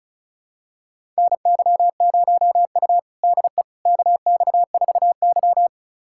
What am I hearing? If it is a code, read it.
Morse code, heard as a keyed tone, about 36 wpm, 700 Hz. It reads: NY0U DE KX4Y
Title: USA Callsign DE Callsign - 4 Characters - 35wpm